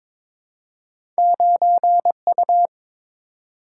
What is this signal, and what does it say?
Morse code, heard as a keyed tone, about 23 wpm, 700 Hz. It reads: 9U
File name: practice / 22wpm / Two Letter-Number Combinations - 22wpm